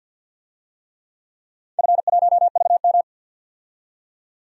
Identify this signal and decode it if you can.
Morse code, heard as a keyed tone, about 52 wpm, 700 Hz. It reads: F1VM
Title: DXCC-100 Call Signs - 4 Characters - 50wpm